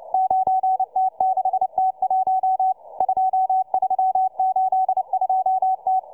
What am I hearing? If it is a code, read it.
Morse code, heard as a keyed tone, about 29 wpm, 760 Hz, with radio noise. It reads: 9T6T1 2382T